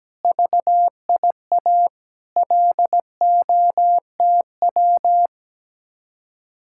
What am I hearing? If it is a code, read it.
Morse code, heard as a keyed tone, about 17 wpm, 700 Hz. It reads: VIA LOTW
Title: Speed-Racing Common QSO Phrases - 17wpm